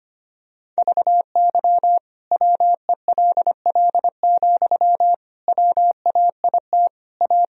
Morse code, heard as a keyed tone, about 25 wpm, 700 Hz. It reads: VY WELL, WAIT A
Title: Sentences from Top 500 Words and Abbreviations 25wpm